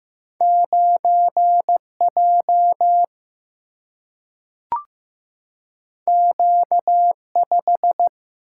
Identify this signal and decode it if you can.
Morse code, heard as a keyed tone, about 15 wpm, 700 Hz. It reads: 9 J Q 5
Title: Single-Letter Single-Number 15wpm